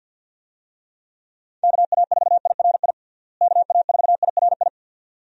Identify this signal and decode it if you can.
Morse code, heard as a keyed tone, about 52 wpm, 700 Hz. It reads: KA4IRI KA4IRI
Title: USA Call Signs - 6 Characters - Repeated 2x - 50wpm